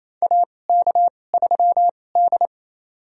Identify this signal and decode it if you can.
Morse code, heard as a keyed tone, about 27 wpm, spaced as for 20 wpm, 700 Hz. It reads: AK3D
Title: USA Call Signs - 4 Characters - 2x Character Spacing - 28wpm